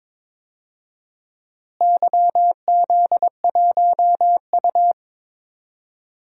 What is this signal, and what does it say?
Morse code, heard as a keyed tone, about 21 wpm, 700 Hz. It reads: YZ1U
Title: DXCC-100 Call Signs - 4 Characters - Repeated 2x - 22wpm